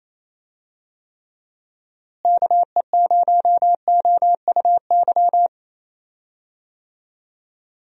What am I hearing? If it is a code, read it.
Morse code, heard as a keyed tone, about 28 wpm, 700 Hz. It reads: KE0OUY
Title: USA Call Signs - 6 Characters - Repeated 3x - 28wpm